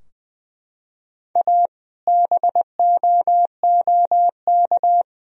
Morse code, heard as a keyed tone, about 20 wpm, 700 Hz. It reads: A BOOK
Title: Sentences from Top 1000 Words - Review Part 2 - 20wpm